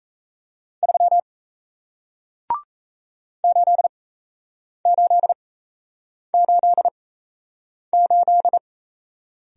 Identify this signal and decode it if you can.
Morse code, heard as a keyed tone, about 38 wpm, spaced as for 8 wpm, 700 Hz. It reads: 3 8888